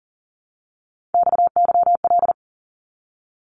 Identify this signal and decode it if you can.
Morse code, heard as a keyed tone, about 40 wpm, 700 Hz. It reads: XYL
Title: Abbreviations 40wpm